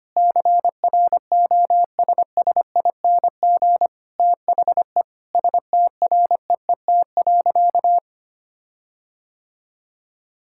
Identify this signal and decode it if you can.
Morse code, heard as a keyed tone, about 25 wpm, 700 Hz. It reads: CROSSING THE STREET.
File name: practice / 25wpm / Difficult Sentences with Hints - 25wpm